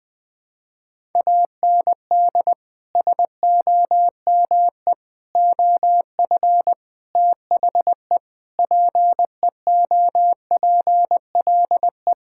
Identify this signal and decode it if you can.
Morse code, heard as a keyed tone, about 20 wpm, 700 Hz. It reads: AND SOME OF THE PEOPLE
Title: Sentences with 4-Word Phrases - 20wpm